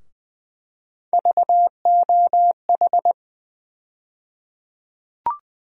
Morse code, heard as a keyed tone, about 20 wpm, 700 Hz. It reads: VOH
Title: USA Call Sign Postfixes - 3 Letters 20wpm